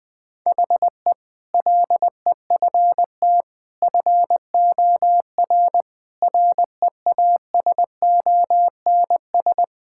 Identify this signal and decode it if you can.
Morse code, heard as a keyed tone, about 20 wpm, 700 Hz. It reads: HE LEFT FOR REASONS
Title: Sentences with 8-Word Phrases - 20wpm